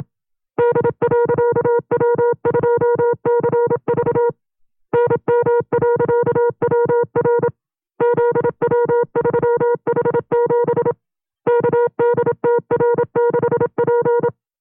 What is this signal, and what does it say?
Morse code, heard as a keyed tone, about 27 wpm, 480 Hz. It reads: D.W2CV NM.WR ZW3H7 KDTR6P
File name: 25WPM